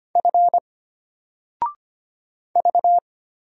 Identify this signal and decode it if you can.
Morse code, heard as a keyed tone, about 25 wpm, 700 Hz. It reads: F V
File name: Single-Letter-Number-25wpm